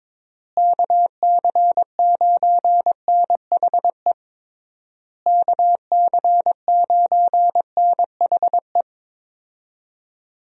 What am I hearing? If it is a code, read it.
Morse code, heard as a keyed tone, about 21 wpm, 700 Hz. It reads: KC9NHE KC9NHE
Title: USA Call Signs - 6 Characters - Repeated 3x - 22wpm